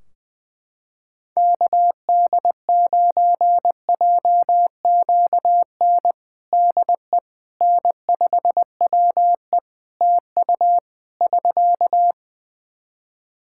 Morse code, heard as a keyed tone, about 20 wpm, 700 Hz. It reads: KD9JQN DE N5WE TU <SK>
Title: Practice QSO - Exchange 4 - Ending - 20wpm